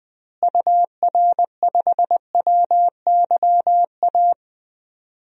Morse code, heard as a keyed tone, about 20 wpm, 700 Hz. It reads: UR5WYA
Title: DXCC-100 Call Signs - 6 Characters - Repeated 2x - 20wpm